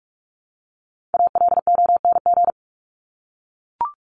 Morse code, heard as a keyed tone, about 44 wpm, 700 Hz. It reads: ALONG
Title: Top 501-750 Words 45wpm